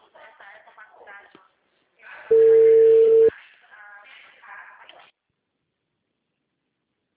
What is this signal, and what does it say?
Ringback tone (China, 450 Hz).